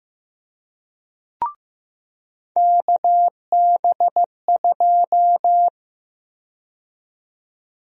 Morse code, heard as a keyed tone, about 15 wpm, 700 Hz. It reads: KB2
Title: USA Call Sign Prefixes - 15wpm